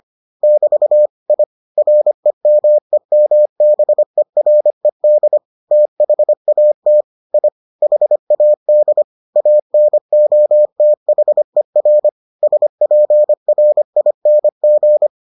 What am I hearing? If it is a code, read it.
Morse code, heard as a keyed tone, about 25 wpm, 600 Hz. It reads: <BT> I REMEMBERED THAT I HAD ANOTHER SPRING